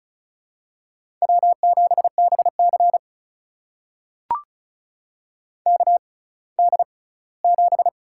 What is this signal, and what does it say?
Morse code, heard as a keyed tone, about 34 wpm, spaced as for 20 wpm, 700 Hz. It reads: W7BC KD7
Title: USA Call Signs - 4 Characters - 6x Character Spacing - 35wpm